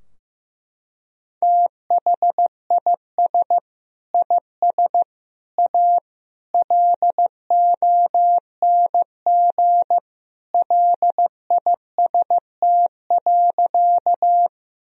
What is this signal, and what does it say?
Morse code, heard as a keyed tone, about 15 wpm, 700 Hz. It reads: THIS IS A LONG LIST.
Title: Sentences from Top 1000 Words - Review Part 2 - 15wpm